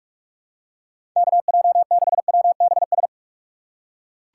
Morse code, heard as a keyed tone, about 46 wpm, 700 Hz. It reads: KJ6WBS
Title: USA Call Signs - 6 Characters - 4x Character Spacing - 45wpm